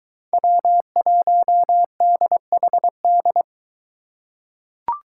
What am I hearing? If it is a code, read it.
Morse code, heard as a keyed tone, about 23 wpm, 700 Hz. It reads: W1DHD